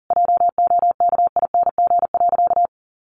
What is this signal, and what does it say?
Morse code, heard as a keyed tone, about 40 wpm, 700 Hz. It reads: JOKING.
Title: A Sound Of Thunder 40wpm